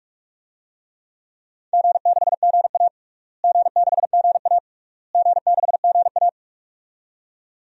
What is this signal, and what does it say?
Morse code, heard as a keyed tone, about 45 wpm, 700 Hz. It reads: G6GA G6GA G6GA
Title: DXCC-100 Call Signs - 4 Characters - Repeated 3x - 45wpm